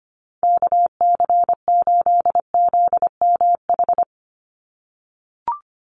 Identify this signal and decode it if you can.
Morse code, heard as a keyed tone, about 25 wpm, 690 Hz. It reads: KC8ZMH
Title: USA Call Signs 25wpm